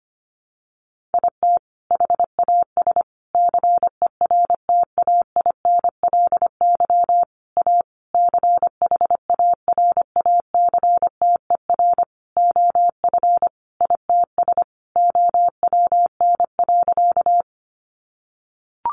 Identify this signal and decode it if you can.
Morse code, heard as a keyed tone, about 25 wpm, 700 Hz. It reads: IT HAS CERTAINLY A CHARACTER OF ITS OWN.
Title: Sentences from Top 1200 Words 25wpm